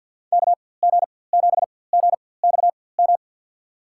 Morse code, heard as a keyed tone, about 50 wpm, spaced as for 23 wpm, 700 Hz. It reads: KG7GXM
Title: USA Call Signs - 6 Characters - 4x Character Spacing - 50wpm